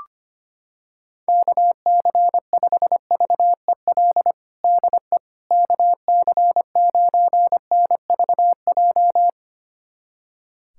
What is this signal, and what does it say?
Morse code, heard as a keyed tone, about 25 wpm, 700 Hz. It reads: KC5VEL DE KC9NVJ
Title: USA Callsign DE Callsign - 6 Characters - 25wpm